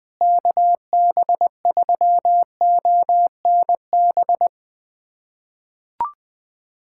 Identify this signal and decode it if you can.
Morse code, heard as a keyed tone, about 20 wpm, 700 Hz. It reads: KB3ONB
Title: USA Callsign DE Callsign - 6 Characters - 20wpm